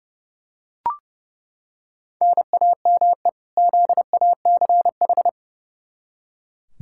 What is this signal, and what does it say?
Morse code, heard as a keyed tone, about 30 wpm, 700 Hz. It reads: NAME ZACH
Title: K1USN SST QSO Practice 30wpm